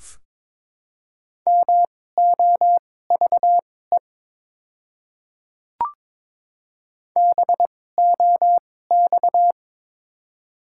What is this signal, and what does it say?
Morse code, heard as a keyed tone, about 22 wpm, spaced as for 16 wpm, 700 Hz. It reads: MOVE BOX
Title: 2x Character Spacing - Top 500 Words - 22wpm